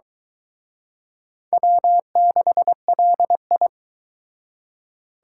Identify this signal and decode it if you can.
Morse code, heard as a keyed tone, about 23 wpm, 700 Hz. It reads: W6LI